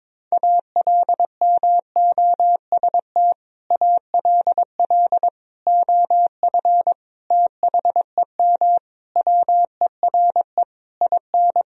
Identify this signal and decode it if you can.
Morse code, heard as a keyed tone, about 22 wpm, 700 Hz. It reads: ALMOST ALL OF THEM WERE IN
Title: Sentences from Top 1000 Words - Review Part 4 - 22wpm